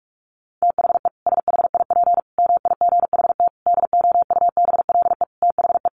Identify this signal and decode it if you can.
Morse code, heard as a keyed tone, about 44 wpm, 700 Hz. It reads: THE SHIP MIGHT DOUBLE THE
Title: Sentences from Top 1400 Words 45wpm